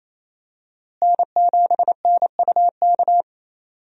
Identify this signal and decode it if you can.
Morse code, heard as a keyed tone, about 28 wpm, 700 Hz. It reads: N7NUK
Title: USA Call Signs - 5 Characters - Repeated 3x - 28wpm